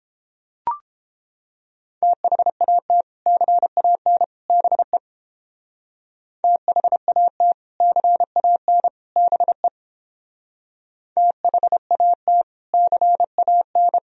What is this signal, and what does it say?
Morse code, heard as a keyed tone, about 29 wpm, 700 Hz. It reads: THAT CAN BE THAT CAN BE THAT CAN